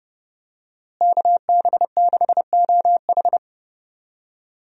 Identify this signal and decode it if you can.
Morse code, heard as a keyed tone, about 30 wpm, 700 Hz. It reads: KB6OH
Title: USA Call Signs - 5 Characters - Repeated 2x - 30wpm